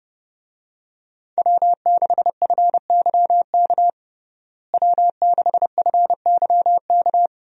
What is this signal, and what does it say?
Morse code, heard as a keyed tone, about 32 wpm, 700 Hz. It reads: W6FYK W6FYK
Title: USA Call Signs - 5 Characters - Repeated 2x - 30wpm